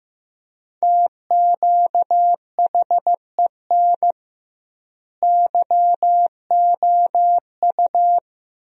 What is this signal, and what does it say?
Morse code, heard as a keyed tone, about 15 wpm, 700 Hz. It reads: TQHEN YOU
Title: 2x Word-Spacing - Sentences from Top 100 Words - 15wpm